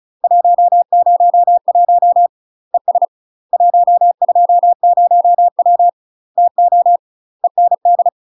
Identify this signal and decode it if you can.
Morse code, heard as a keyed tone, about 36 wpm, 700 Hz. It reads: 101 ES 120W TO END
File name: UK MASTER QSOs TRAINING / 35WPM / UK MASTER QSOs 0016